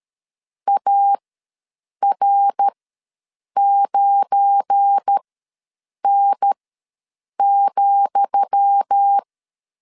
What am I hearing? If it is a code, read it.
Morse code, heard as a keyed tone, about 13 wpm, spaced as for 7 wpm, 800 Hz. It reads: AR9N,
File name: LEZIONI / 10 SIGNS / 10D COMA ws 12wpm 3618